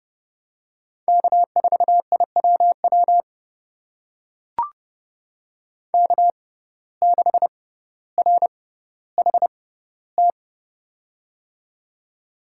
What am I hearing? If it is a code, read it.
Morse code, heard as a keyed tone, about 32 wpm, spaced as for 16 wpm, 700 Hz. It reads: K4IWW K6RHT
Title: USA Call Signs - 5 Characters - 6x Character Spacing - 30wpm